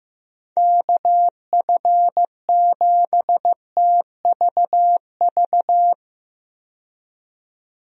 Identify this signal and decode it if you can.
Morse code, heard as a keyed tone, about 15 wpm, 700 Hz. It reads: KF7TVV
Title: USA Call Signs - 6 Characters - 15wpm